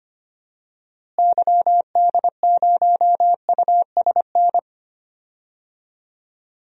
Morse code, heard as a keyed tone, about 25 wpm, 700 Hz. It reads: YD0USN
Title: DXCC-100 Call Signs - 6 Characters - Repeated 3x - 25wpm